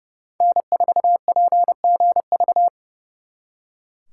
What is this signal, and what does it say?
Morse code, heard as a keyed tone, about 40 wpm, 700 Hz. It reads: N4PGV